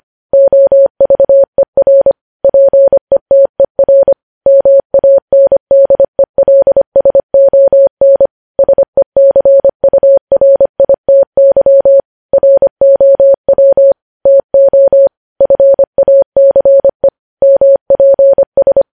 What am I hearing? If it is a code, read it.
Morse code, heard as a keyed tone, about 25 wpm, 550 Hz. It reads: OVER PETER MANDELSON SECURITY ROW TO FACE MPS
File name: podcast / News headlines-1x-25x25